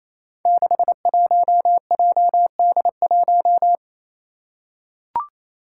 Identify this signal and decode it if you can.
Morse code, heard as a keyed tone, about 28 wpm, 700 Hz. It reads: 61JD1
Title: Five Challenging Letter-Number Combinations 28wpm